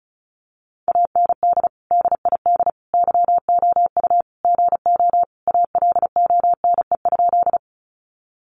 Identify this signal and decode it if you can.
Morse code, heard as a keyed tone, about 34 wpm, 700 Hz. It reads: AND DID YOU GO ALONE?
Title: Sentences from Top 1000 Words 35wpm